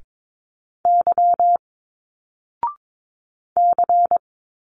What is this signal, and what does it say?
Morse code, heard as a keyed tone, about 22 wpm, 700 Hz. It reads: Y C
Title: Single Letters 22wpm